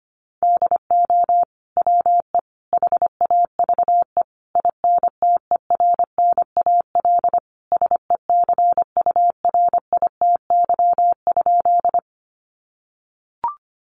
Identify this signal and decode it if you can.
Morse code, heard as a keyed tone, about 25 wpm, 700 Hz. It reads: DO WE HAVE INTERNAL SECURITY?
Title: Sentences from Top 1400 Words 25wpm